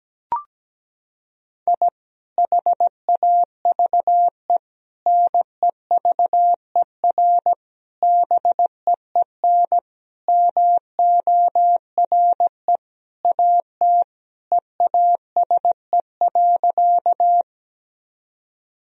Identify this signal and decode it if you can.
Morse code, heard as a keyed tone, about 17 wpm, 700 Hz. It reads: I HAVE NEVER BEEN MORE AT EASE.
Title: Sentences from Top 1000 Words - Review Part 1 - 17wpm